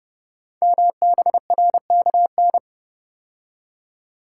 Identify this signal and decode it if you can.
Morse code, heard as a keyed tone, about 30 wpm, 700 Hz. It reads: MBRKN
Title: Code Groups - 5 Letters - 30wpm